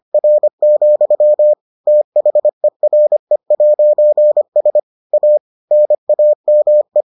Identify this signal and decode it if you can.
Morse code, heard as a keyed tone, about 25 wpm, 600 Hz. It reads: R, THERE'S A NAME